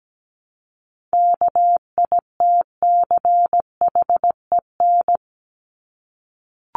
Morse code, heard as a keyed tone, about 17 wpm, 700 Hz. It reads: KITCHEN
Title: Top 1251-1500 Words 17wpm